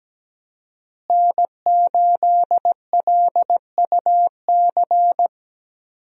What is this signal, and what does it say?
Morse code, heard as a keyed tone, about 17 wpm, 700 Hz. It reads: N8LUC
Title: USA Call Signs - 5 Characters - Repeated 2x - 17wpm